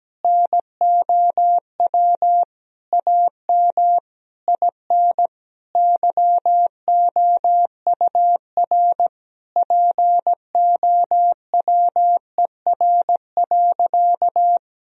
Morse code, heard as a keyed tone, about 17 wpm, 700 Hz. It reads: NOW AM IN YOUR POWER.
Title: Sentences from Top 1000 Words - Review Part 2 - 17wpm